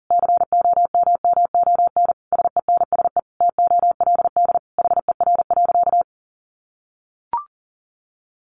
Morse code, heard as a keyed tone, about 40 wpm, 700 Hz. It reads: COMMON SENSE TOLD HER.
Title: Sentences from Top 1200 Words 40wpm